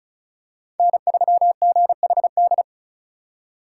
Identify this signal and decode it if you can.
Morse code, heard as a keyed tone, about 34 wpm, 700 Hz. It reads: N3GHD
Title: USA Call Signs - 5 Characters - 6x Character Spacing - 35wpm